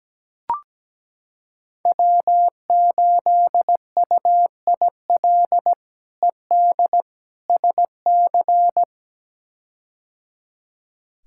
Morse code, heard as a keyed tone, about 17 wpm, 700 Hz. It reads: W8UIL ED SC
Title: K1USN SST QSO Practice - 17wpm